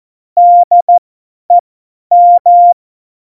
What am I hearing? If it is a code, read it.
Morse code, heard as a keyed tone, about 14 wpm, spaced as for 10 wpm, 700 Hz. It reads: DEM
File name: Morsekurs / Witze / Witz 11. September 2018, 50 F70